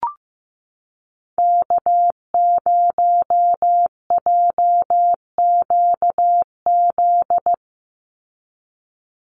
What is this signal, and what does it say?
Morse code, heard as a keyed tone, about 15 wpm, 700 Hz. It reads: K0JQZ
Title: NA SOTA Activator Callsigns 15wpm